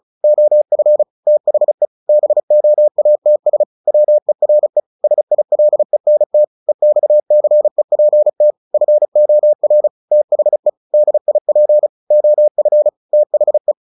Morse code, heard as a keyed tone, about 35 wpm, 600 Hz. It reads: OF THE BOATS WERE SILENT EXCEPT FOR THE DIP OF THE
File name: the old man and the sea / the old man and the sea 1 35wpm